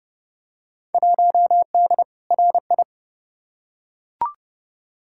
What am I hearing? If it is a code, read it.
Morse code, heard as a keyed tone, about 30 wpm, 700 Hz. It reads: 1D RI
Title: Field Day Exchanges - 30wpm